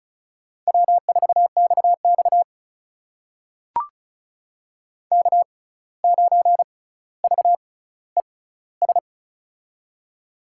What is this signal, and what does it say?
Morse code, heard as a keyed tone, about 36 wpm, spaced as for 17 wpm, 700 Hz. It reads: W4XX K9VES
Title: USA Call Signs - 6x Character Spacing - 35wpm